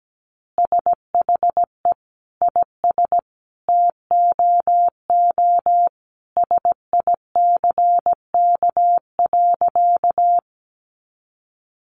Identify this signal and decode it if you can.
Morse code, heard as a keyed tone, about 17 wpm, 700 Hz. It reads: SHE IS TOO SICK.
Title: Easy Phrases 17wpm